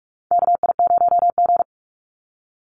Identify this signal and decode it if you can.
Morse code, heard as a keyed tone, about 46 wpm, 700 Hz. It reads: KI0G
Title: NA SOTA Chaser Callsigns 45wpm